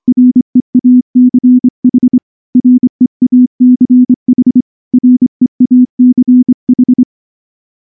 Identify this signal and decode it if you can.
Morse code, heard as a keyed tone, about 24 wpm, 250 Hz. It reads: REACH REACH REACH
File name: morse / morse-training-25WPM